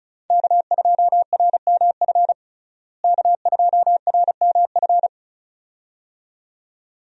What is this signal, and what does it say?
Morse code, heard as a keyed tone, about 36 wpm, 700 Hz. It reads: K2RMF K2RMF
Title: USA Call Signs - 5 Characters - Repeated 2x - 35wpm